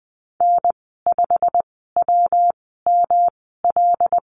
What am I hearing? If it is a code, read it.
Morse code, heard as a keyed tone, about 19 wpm, spaced as for 14 wpm, 700 Hz. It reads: N5WML
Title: 179 Lesson 29g - Slash - Call Signs 20wpm